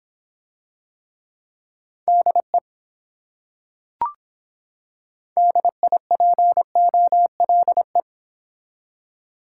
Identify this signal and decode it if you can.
Morse code, heard as a keyed tone, about 25 wpm, 700 Hz. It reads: DE DIPOLE